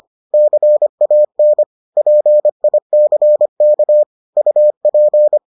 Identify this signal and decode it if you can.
Morse code, heard as a keyed tone, about 25 wpm, 600 Hz. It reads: CAN PICK UP